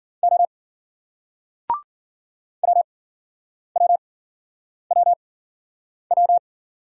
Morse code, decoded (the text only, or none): O WWWW